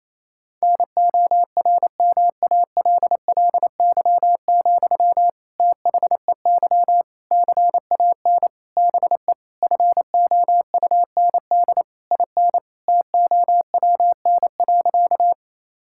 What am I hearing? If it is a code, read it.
Morse code, heard as a keyed tone, about 27 wpm, 700 Hz. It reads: NORMALLY, THEY CAN BE FOUND IN TOWN.
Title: Sentences from Top 1000 Words - Review Part 6 - 28wpm